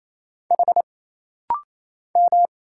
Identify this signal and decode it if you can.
Morse code, heard as a keyed tone, about 27 wpm, 700 Hz. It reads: H M
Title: Single Letter-Number Alt - Rapid-Fire 28wpm